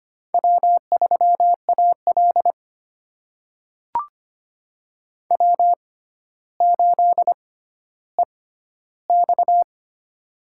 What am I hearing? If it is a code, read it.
Morse code, heard as a keyed tone, about 25 wpm, spaced as for 13 wpm, 700 Hz. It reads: W3AL W8EX
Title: USA Call Signs - 4 Characters - 6x Character Spacing - 25wpm